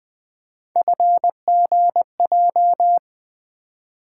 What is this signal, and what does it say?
Morse code, heard as a keyed tone, about 20 wpm, 700 Hz. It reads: FGJ
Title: Three Letter-Number Combinations - 20wpm